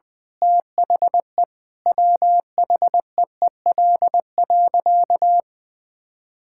Morse code, heard as a keyed tone, about 20 wpm, 700 Hz. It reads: THE WHEEL.
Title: Sentences from Top 1000 Words - Review Part 5 - 20wpm